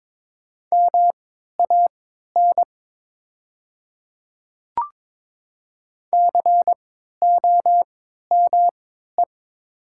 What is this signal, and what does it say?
Morse code, heard as a keyed tone, about 23 wpm, spaced as for 13 wpm, 700 Hz. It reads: MAN COME
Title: 3x Character Spacing - Top 300 Words - 22wpm